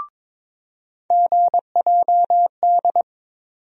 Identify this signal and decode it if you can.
Morse code, heard as a keyed tone, about 22 wpm, 700 Hz. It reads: GJD
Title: USA Call Sign Postfixes - 3 Letters 22wpm